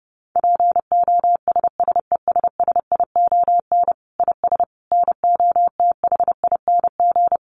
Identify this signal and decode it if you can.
Morse code, heard as a keyed tone, about 30 wpm, 700 Hz. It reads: POSSESSION IS NOTHING
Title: Sentences from Top 1400 Words 30wpm